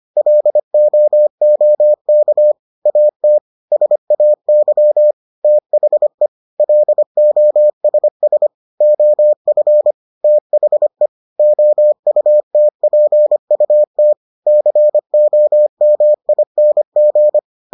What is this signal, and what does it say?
Morse code, heard as a keyed tone, about 25 wpm, 600 Hz. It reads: LOOK AT SAY THE LOSS OF THE OUTPUT COMING